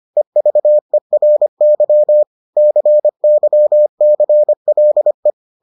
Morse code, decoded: EVERY CYCLE

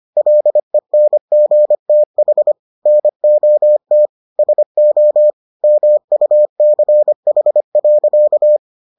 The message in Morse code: LENGTH NOT SO MUCH.

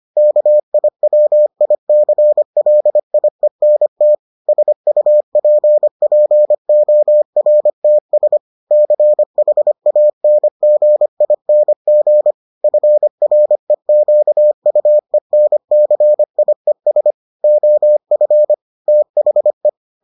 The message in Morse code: KIWICLIENT SUPPORTS CHANGING FREQUENCIES OF THE

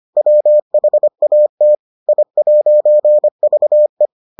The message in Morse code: WHAT I'VE

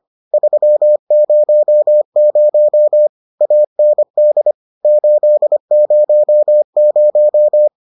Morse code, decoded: 300 AND 800

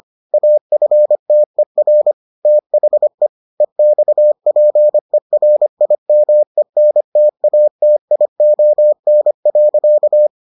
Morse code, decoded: AFTER THE EXPERIMENTATION.

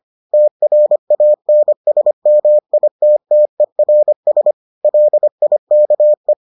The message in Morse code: TRANSMITTERS LIKE

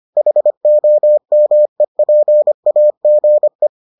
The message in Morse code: HOMEPAGE

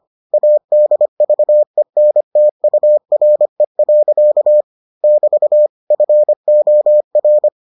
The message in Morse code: ADVENTURE. <BT> FOR

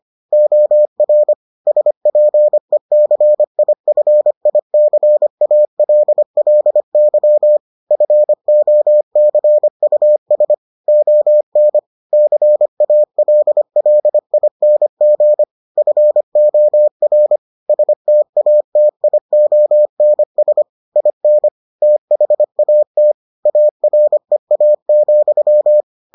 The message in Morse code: OR SPECIFICALLY FOCUS ON CALLING FOR STATIONS IN THAT AREA,

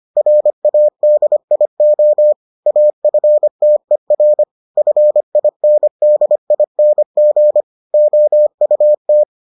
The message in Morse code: RADIO AFTER FINDING OUT

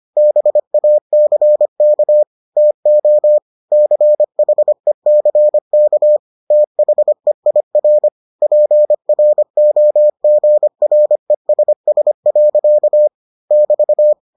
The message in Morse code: BACK TO CHECK THEIR PROGRESS. <BT>

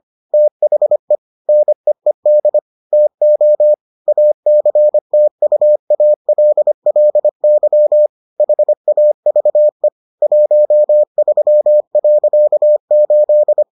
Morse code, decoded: THE NEED TO ACTUALLY HAVE 13.8

